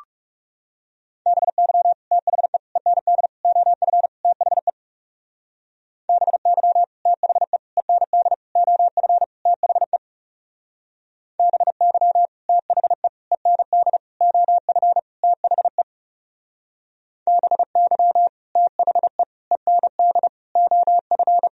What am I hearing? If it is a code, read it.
Morse code, heard as a keyed tone, about 37 wpm, 700 Hz. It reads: BY THE END OF THE BY THE END OF THE BY THE END OF THE BY THE END OF
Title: Speed-Racing - Sets of 5 Words 30wpm